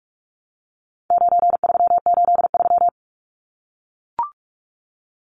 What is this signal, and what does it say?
Morse code, heard as a keyed tone, about 44 wpm, 700 Hz. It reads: 9383
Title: Numbers - 4-Digit 45wpm